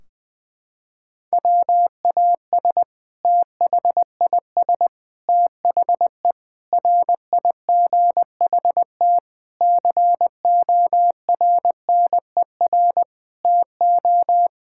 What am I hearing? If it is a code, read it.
Morse code, heard as a keyed tone, about 21 wpm, 700 Hz. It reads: WAS THIS THE RIGHT CORNER TO
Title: Sentences from Top 1600 Words 20wpm